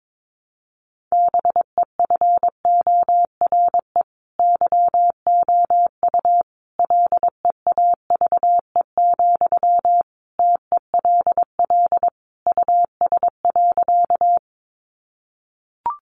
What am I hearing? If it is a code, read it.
Morse code, heard as a keyed tone, about 22 wpm, 700 Hz. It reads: BEFORE YOU LEAVE, TELL US.
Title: Sentences from Top 400 Words 22wpm